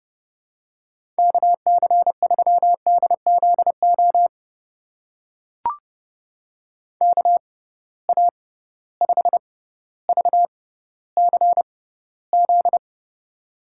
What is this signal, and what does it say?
Morse code, heard as a keyed tone, about 32 wpm, spaced as for 16 wpm, 700 Hz. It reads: KC3DZO KA5VCZ
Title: USA Call Signs - 6 Characters - 6x Character Spacing - 30wpm